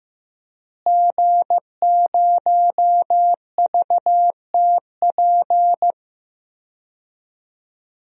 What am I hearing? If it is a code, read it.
Morse code, heard as a keyed tone, about 15 wpm, 700 Hz. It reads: G0VTP
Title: DXCC-100 Call Signs - 5 Characters - Repeated 3x - 15wpm